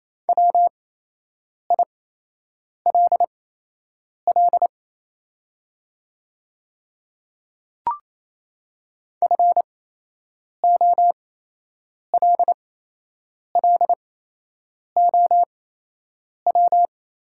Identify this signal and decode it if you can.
Morse code, heard as a keyed tone, about 28 wpm, spaced as for 8 wpm, 700 Hz. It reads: WILL FOLLOW
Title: 8x Character Spacing - Top 200 Words - 28wpm